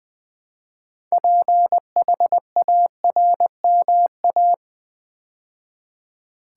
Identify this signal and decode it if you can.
Morse code, heard as a keyed tone, about 20 wpm, 700 Hz. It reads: PHARMA
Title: Build A Word - 2 Letters At A Time - 8-Letter Words - 20wpm